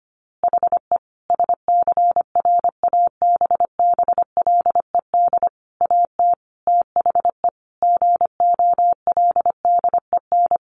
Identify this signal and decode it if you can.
Morse code, heard as a keyed tone, about 25 wpm, 700 Hz. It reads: HE SCRABBLED AT THE GOLDEN
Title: A Sound Of Thunder 25wpm